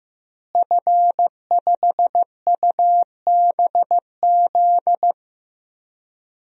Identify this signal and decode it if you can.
Morse code, heard as a keyed tone, about 15 wpm, 700 Hz. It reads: F5UBZ
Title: DXCC-100 Call Signs - 5 Characters - Repeated 3x - 15wpm